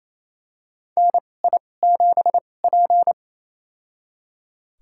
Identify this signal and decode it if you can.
Morse code, heard as a keyed tone, about 28 wpm, spaced as for 20 wpm, 700 Hz. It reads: NI7P
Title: USA Call Signs - 2x Character Spacing - 28wpm